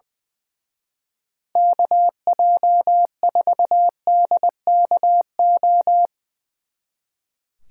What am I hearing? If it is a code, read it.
Morse code, heard as a keyed tone, about 21 wpm, 700 Hz. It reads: KJ4DKO
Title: Speed Racing - USA 2x3 Call Signs 20wpm